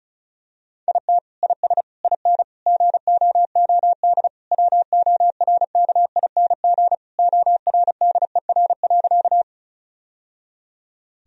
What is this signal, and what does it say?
Morse code, heard as a keyed tone, about 36 wpm, 700 Hz. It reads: IT IS IN GOOD WORKING ORDER.
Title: Sentences from Top 1000 Words - Review Part 4 - 35wpm